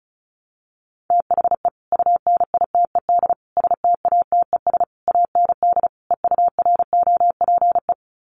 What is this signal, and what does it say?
Morse code, heard as a keyed tone, about 34 wpm, 700 Hz. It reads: THE UNITED STATES AND EUROPE
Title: Sets of 5 Words Encore 35wpm